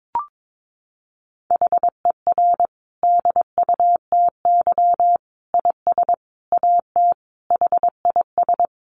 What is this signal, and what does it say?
Morse code, heard as a keyed tone, about 22 wpm, 700 Hz. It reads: HER DUTY IS AT HIS
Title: Sentences from Top 1500 Words 22wpm